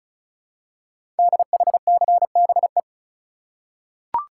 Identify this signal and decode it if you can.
Morse code, heard as a keyed tone, about 36 wpm, 700 Hz. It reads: DHCBE
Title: Code Groups - 5 Letters - 35wpm